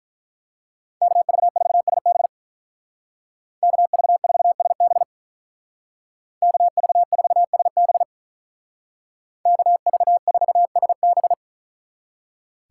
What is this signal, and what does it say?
Morse code, heard as a keyed tone, about 44 wpm, 700 Hz. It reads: KV4SB KV4SB KV4SB KV4SB
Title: Speed Racing - USA 2x2 Call Signs 35wpm